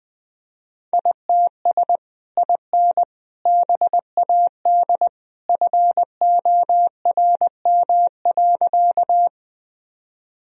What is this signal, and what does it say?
Morse code, heard as a keyed tone, about 21 wpm, 700 Hz. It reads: ITS IN BAD FORM.
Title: Sentences from Top 1000 Words - Review Part 1 - 20wpm